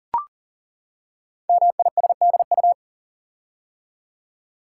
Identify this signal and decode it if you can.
Morse code, heard as a keyed tone, about 42 wpm, 700 Hz. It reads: MISDU
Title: Code Groups - 5 Letters - 40wpm